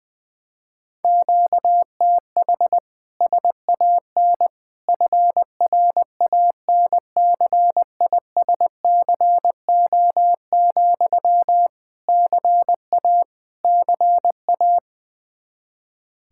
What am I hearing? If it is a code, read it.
Morse code, heard as a keyed tone, about 20 wpm, 700 Hz. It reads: QTH SAN FRANCISCO, CA CA
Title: K1USN SST QSO Practice 20wpm